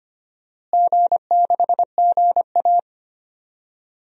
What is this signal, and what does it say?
Morse code, heard as a keyed tone, about 25 wpm, 700 Hz. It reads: G6GA
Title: DXCC-100 Call Signs - 4 Characters - Repeated 2x - 25wpm